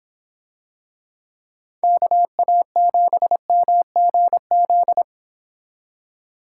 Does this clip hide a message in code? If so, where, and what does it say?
Yes: Morse code from 1.8 to 5.0 s, heard as a keyed tone, about 26 wpm, 700 Hz. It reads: KA7MGZ